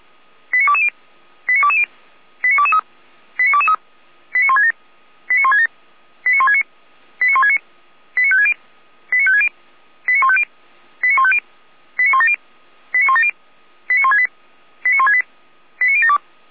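Five-tone selective call (ZVEI), heard as quick runs of five tones, tones of about 70 ms. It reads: 89220 89220 89223 89223 89167 89167 89169 89169 89560 89560 89150 89150 89170 89170 89168 89168 89072
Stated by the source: ZVEI five-tone selective call 89121, received by SDR.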